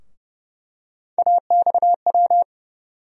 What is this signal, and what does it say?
Morse code, heard as a keyed tone, about 30 wpm, 700 Hz. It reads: AXW
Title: USA Call Sign Postfixes - 3 Letters 30wpm